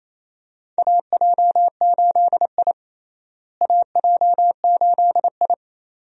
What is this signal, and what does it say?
Morse code, heard as a keyed tone, about 28 wpm, 700 Hz. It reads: AJ8I AJ8I
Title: USA Call Signs - 4 Characters - Repeated 2x - 28wpm